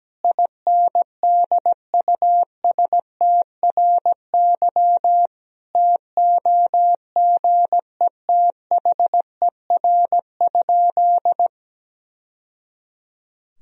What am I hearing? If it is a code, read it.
Morse code, heard as a keyed tone, about 17 wpm, 700 Hz. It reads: INDUSTRY TOGETHER?
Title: Sentences from Top 1000 Words - Review Part 4 - 17wpm